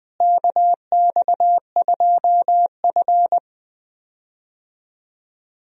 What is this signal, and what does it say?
Morse code, heard as a keyed tone, about 21 wpm, 700 Hz. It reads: KX2F
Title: USA Call Signs - 4 Characters - Repeated 2x - 20wpm